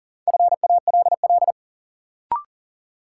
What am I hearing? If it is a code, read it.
Morse code, heard as a keyed tone, about 40 wpm, 700 Hz. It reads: FALL
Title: Sentences from Top 500 Words - New Word Drill - 40wpm